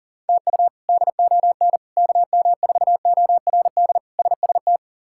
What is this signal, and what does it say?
Morse code, heard as a keyed tone, about 40 wpm, 700 Hz. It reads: TU DON KM4ORD SST
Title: K1USN SST QSO Practice - 40wpm